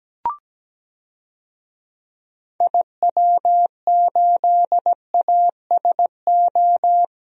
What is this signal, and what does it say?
Morse code, heard as a keyed tone, about 17 wpm, 700 Hz. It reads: IW8ASO
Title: DXCC-100 Call Signs - 6 Characters - Repeated 2x - 17wpm